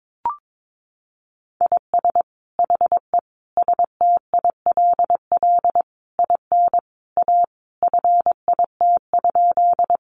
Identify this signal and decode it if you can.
Morse code, heard as a keyed tone, about 22 wpm, 700 Hz. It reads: IS HE STILL IN A FIT?
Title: Sentences from Top 800 Words 22wpm